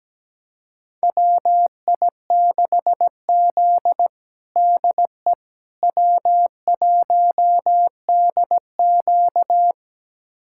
Morse code, heard as a keyed tone, about 17 wpm, 700 Hz. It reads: WI6Z DE W1DQ
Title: USA Callsign DE Callsign - 4 Characters - 17wpm